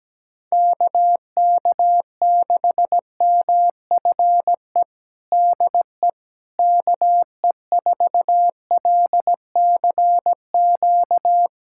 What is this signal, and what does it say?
Morse code, heard as a keyed tone, about 17 wpm, 700 Hz. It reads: KK6MFE DE KE4LCQ
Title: USA Callsign DE Callsign - 6 Characters - 17wpm